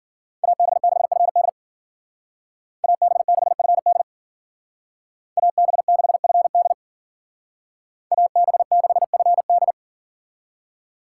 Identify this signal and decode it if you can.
Morse code, heard as a keyed tone, about 49 wpm, 700 Hz. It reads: AB6FD AB6FD AB6FD AB6FD
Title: Speed Racing - USA Call Signs 40wpm